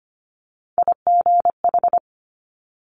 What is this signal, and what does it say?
Morse code, heard as a keyed tone, about 25 wpm, 700 Hz. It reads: IGH
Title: Multi-letter Phonemes Quick Fire 25wpm